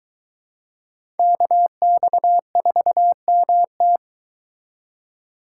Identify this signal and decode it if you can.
Morse code, heard as a keyed tone, about 22 wpm, 700 Hz. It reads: KX4MT